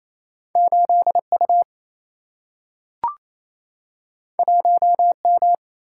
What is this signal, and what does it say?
Morse code, heard as a keyed tone, about 28 wpm, 700 Hz. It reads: 8U 1M
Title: Single-Letter Single-Number 28wpm